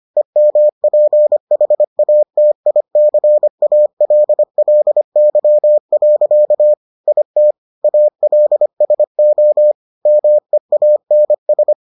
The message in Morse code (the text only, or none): EMPHATICALLY. IT ALSO MEANS